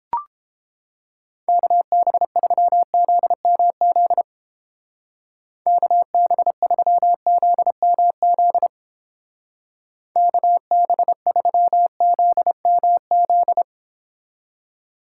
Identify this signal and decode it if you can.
Morse code, heard as a keyed tone, about 31 wpm, 700 Hz. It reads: KB3ZMZ KB3ZMZ KB3ZMZ